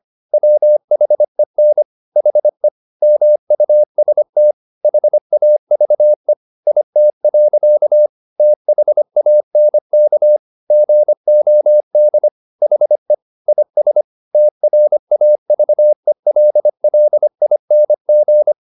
Morse code, heard as a keyed tone, about 25 wpm, 600 Hz. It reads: WHEN HE MUST HAVE IT. THANK GOD HE IS TRAVELLING
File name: the old man and the sea part 1 25wpm